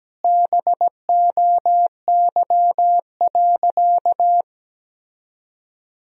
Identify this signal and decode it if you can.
Morse code, heard as a keyed tone, about 17 wpm, 700 Hz. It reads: BOY.
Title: 3x Word-Spacing - Sentences from Top 200 Words - 17wpm